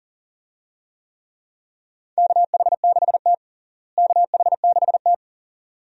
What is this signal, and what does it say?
Morse code, heard as a keyed tone, about 40 wpm, 700 Hz. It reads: KH6T KH6T
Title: DXCC-100 Call Signs - 4 Characters - Repeated 2x - 40wpm